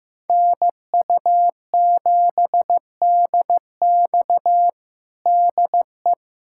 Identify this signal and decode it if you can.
Morse code, heard as a keyed tone, about 15 wpm, 700 Hz. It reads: NU7DX DE
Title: USA Callsign DE Callsign - 5 Characters - 15wpm